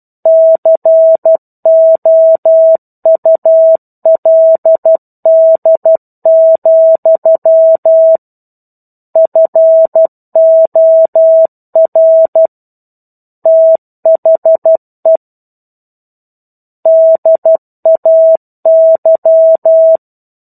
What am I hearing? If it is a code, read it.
Morse code, heard as a keyed tone, about 12 wpm, 650 Hz. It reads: COULD, FOR THE DAY
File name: 01-Chapter-01 12 WPM